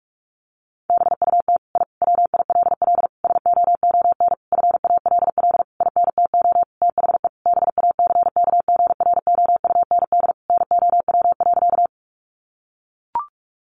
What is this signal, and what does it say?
Morse code, heard as a keyed tone, about 44 wpm, 700 Hz. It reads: BUT I WILL SOON FALL INTO THE BACKGROUND NOW.
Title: Sentences from Top 1300 Words 45wpm